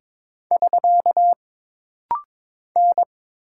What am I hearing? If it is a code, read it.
Morse code, heard as a keyed tone, about 22 wpm, 700 Hz. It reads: <SK> N
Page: This rapid-fire timing is geared to push your ICR (Instant Character Recognition) skills.